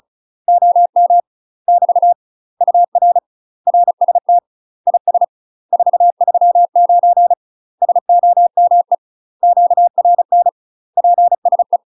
Morse code, heard as a keyed tone, about 36 wpm, 700 Hz. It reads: OM <BT> UR RST IS 439 SOME QRN PSE
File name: UK MASTER QSOs TRAINING / 35WPM / UK MASTER QSOs 0010